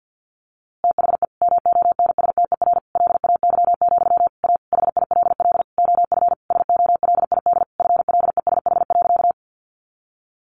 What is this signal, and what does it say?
Morse code, heard as a keyed tone, about 48 wpm, 700 Hz. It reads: THE MONSTER LAY, A HILL OF SOLID FLESH.
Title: A Sound Of Thunder 50wpm